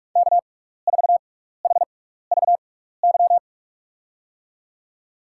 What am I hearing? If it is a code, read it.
Morse code, heard as a keyed tone, about 44 wpm, spaced as for 15 wpm, 700 Hz. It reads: K4HVY
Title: USA Call Signs - 6x Character Spacing - 45wpm